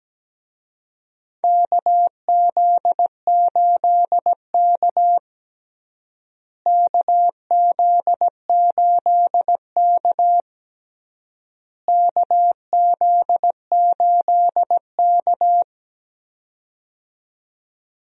Morse code, heard as a keyed tone, about 17 wpm, 700 Hz. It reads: KZ8K KZ8K KZ8K
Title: USA Call Signs - 4 Characters - Repeated 3x - 17wpm